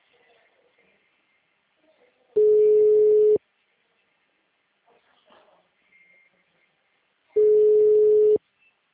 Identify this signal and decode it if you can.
Ringback tone (Europe, 425 Hz).